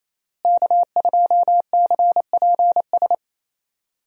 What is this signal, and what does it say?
Morse code, heard as a keyed tone, about 27 wpm, 700 Hz. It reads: K2CPS
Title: USA Call Signs - 5 Characters - Repeated 3x - 28wpm